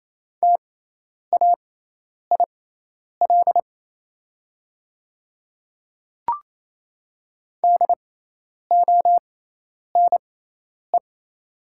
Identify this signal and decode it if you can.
Morse code, heard as a keyed tone, about 28 wpm, spaced as for 10 wpm, 700 Hz. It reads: TAIL DONE
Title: 6x Character Spacing - Top 500 Words - 28wpm